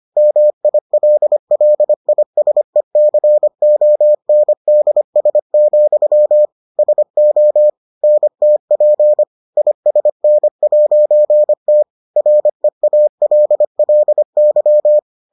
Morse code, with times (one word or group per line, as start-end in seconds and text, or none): 0.2-6.5 MILLISECONDS,
6.8-7.7 SO
8.0-9.2 NTP
9.6-11.8 ISN'T
12.2-15.0 REALLY